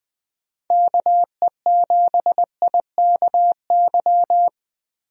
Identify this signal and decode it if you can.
Morse code, heard as a keyed tone, about 20 wpm, 700 Hz. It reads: KE7IKY
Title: USA Call Signs - 6 Characters - Repeated 2x - 20wpm